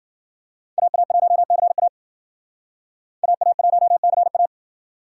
Morse code, heard as a keyed tone, about 53 wpm, 700 Hz. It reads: AA1CA AA1CA